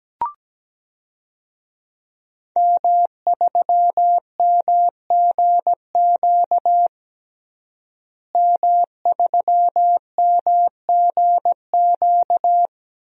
Morse code, heard as a keyed tone, about 17 wpm, 700 Hz. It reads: M3MGQ M3MGQ
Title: DXCC-100 Call Signs - 5 Characters - Repeated 3x - 17wpm